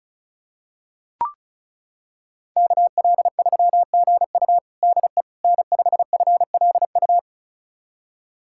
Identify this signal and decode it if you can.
Morse code, heard as a keyed tone, about 36 wpm, 700 Hz. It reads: KL3GU DE N5FLU
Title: USA Callsign DE Callsign - 5 Characters - 35wpm